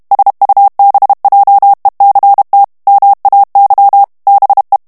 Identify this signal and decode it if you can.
Morse code, heard as a keyed tone, about 32 wpm, 800 Hz. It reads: SUBJECT MAY BE
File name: morse mp3 / ANGLAIS texte clair 3 vitesse 30 mots minute